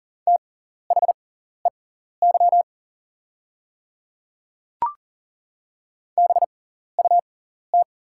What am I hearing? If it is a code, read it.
Morse code, heard as a keyed tone, about 38 wpm, spaced as for 14 wpm, 690 Hz. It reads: THEY BUT
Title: 6x Character Spacing - Top 100 Words - 40wpm